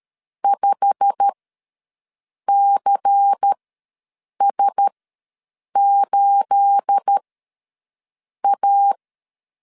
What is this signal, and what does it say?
Morse code, heard as a keyed tone, about 13 wpm, spaced as for 6 wpm, 800 Hz. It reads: 5CS8A
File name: LEZIONI / 10 SIGNS / 10C DOT ws 12wpm 3548